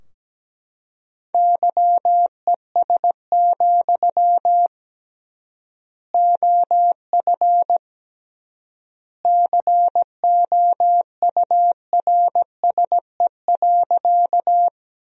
Morse code, heard as a keyed tone, about 17 wpm, 700 Hz. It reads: YES, OF COURSE.
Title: 3x Word-Spacing - Sentences from Top 500 Words - 17wpm